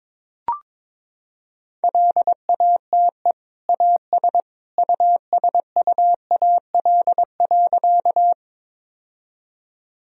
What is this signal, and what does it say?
Morse code, heard as a keyed tone, about 21 wpm, 700 Hz. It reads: LATE AS USUAL.
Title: Sentences from Top 1000 Words - Review Part 2 - 22wpm